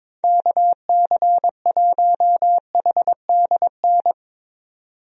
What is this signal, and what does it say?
Morse code, heard as a keyed tone, about 22 wpm, 700 Hz. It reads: KC1HDN
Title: USA Call Signs - 6 Characters - 22wpm